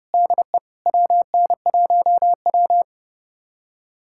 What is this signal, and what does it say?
Morse code, heard as a keyed tone, about 32 wpm, 700 Hz. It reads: DE WN1W
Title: USA Callsign DE Callsign - 4 Characters - 30wpm